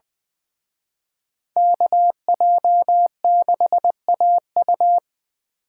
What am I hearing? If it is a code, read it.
Morse code, heard as a keyed tone, about 25 wpm, 700 Hz. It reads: KJ6AU
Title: Speed Racing - USA Call Signs 20wpm